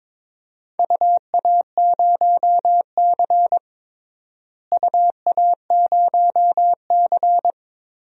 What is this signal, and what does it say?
Morse code, heard as a keyed tone, about 22 wpm, 700 Hz. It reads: UA0C UA0C
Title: DXCC-100 Call Signs - 4 Characters - Repeated 3x - 22wpm